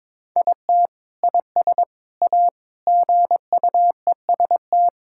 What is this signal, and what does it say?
Morse code, heard as a keyed tone, about 22 wpm, 700 Hz. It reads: IT IS A GUEST
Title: Sentences from Top 1600 Words 22wpm